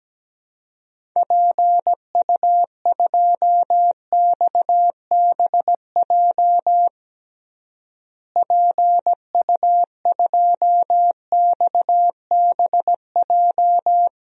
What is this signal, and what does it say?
Morse code, heard as a keyed tone, about 17 wpm, 700 Hz. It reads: PU2XBJ PU2XBJ
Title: DXCC-100 Call Signs - 6 Characters - Repeated 2x - 17wpm